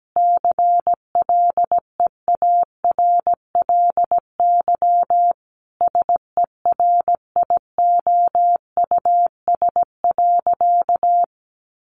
Morse code, heard as a keyed tone, about 17 wpm, 700 Hz. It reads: CLEARLY SERIOUS.
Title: Sentences from Top 1300 Words 17wpm